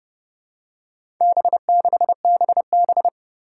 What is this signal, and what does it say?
Morse code, heard as a keyed tone, about 30 wpm, 700 Hz. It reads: B6BB
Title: Four Challenging Letter-Number Combinations 30wpm